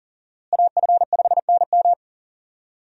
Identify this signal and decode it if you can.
Morse code, heard as a keyed tone, about 40 wpm, 700 Hz. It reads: AF5NM
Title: USA Call Signs - 5 Characters - Repeated 2x - 40wpm